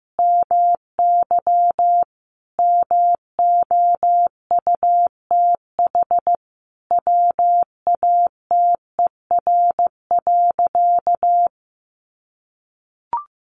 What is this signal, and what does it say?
Morse code, heard as a keyed tone, about 15 wpm, 700 Hz. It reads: MY MOUTH WATER.
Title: Sentences from Top 1500 Words 15wpm